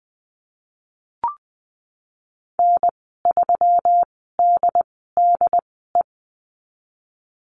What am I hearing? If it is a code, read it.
Morse code, heard as a keyed tone, about 20 wpm, spaced as for 15 wpm, 700 Hz. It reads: N3DDE
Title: 167 Lesson 27g - Number 7 - Call Signs 20wpm